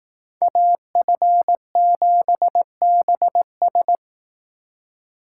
Morse code, heard as a keyed tone, about 19 wpm, 700 Hz. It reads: AF7BS